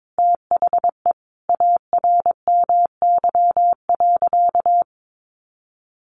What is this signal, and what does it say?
Morse code, heard as a keyed tone, about 22 wpm, 700 Hz. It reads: THE ARMY.
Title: Sentences from Top 1400 Words 22wpm